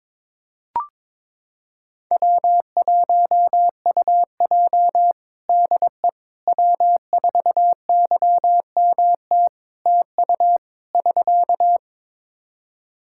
Morse code, heard as a keyed tone, about 22 wpm, 700 Hz. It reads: W1UJ DE W4YMT TU <SK>
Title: Practice QSO - Exchange 4 - Ending - 22wpm